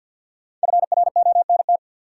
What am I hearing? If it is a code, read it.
Morse code, heard as a keyed tone, about 50 wpm, 700 Hz. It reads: FRONT
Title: Sets of 2 Words - New Word Drill - 50wpm